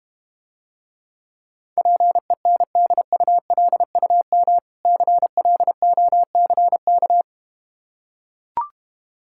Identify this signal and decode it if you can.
Morse code, heard as a keyed tone, about 32 wpm, 700 Hz. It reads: PENDULUM CLOCK